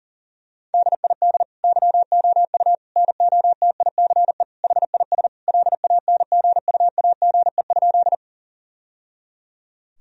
Morse code, heard as a keyed tone, about 40 wpm, 700 Hz. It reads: DID YOU NOTICE HIS LANGUAGE?
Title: Sentences from Top 1000 Words - Review Part 3 - 40wpm